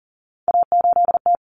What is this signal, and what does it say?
Morse code, heard as a keyed tone, about 40 wpm, 700 Hz. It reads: A8T
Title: Common Cut Numbers - 3-Digit 40wpm